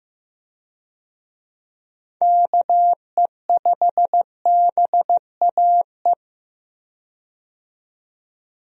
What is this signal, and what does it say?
Morse code, heard as a keyed tone, about 15 wpm, 700 Hz. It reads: KE5BAE
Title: USA Call Signs - 6 Characters - 15wpm